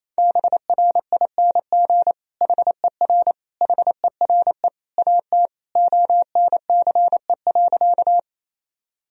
Morse code, heard as a keyed tone, about 27 wpm, 700 Hz. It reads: BRING HER HERE AT ONCE.
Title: Sentences from Top 1000 Words - Review Part 3 - 28wpm